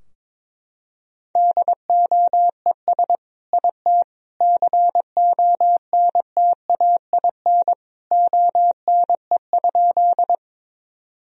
Morse code, heard as a keyed tone, about 22 wpm, 700 Hz. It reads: DOES IT CONTAIN ONE?
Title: Sentences from Top 1000 Words - Review Part 2 - 22wpm